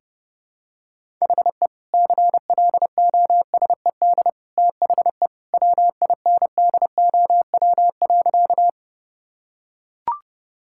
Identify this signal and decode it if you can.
Morse code, heard as a keyed tone, about 31 wpm, 700 Hz. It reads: HE CLOSED THE WINDOW.
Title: Sentences from Top 1600 Words 30wpm